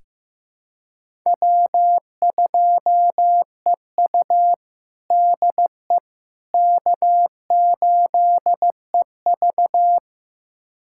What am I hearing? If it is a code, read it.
Morse code, heard as a keyed tone, about 15 wpm, 700 Hz. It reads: W2EU DE K8EV
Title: USA Callsign DE Callsign - 4 Characters - 15wpm